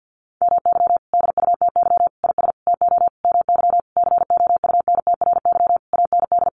Morse code, heard as a keyed tone, about 50 wpm, 700 Hz. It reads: MY DUTY IS TO MY COUNTRY AND
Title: Sentences from Top 1500 Words 50wpm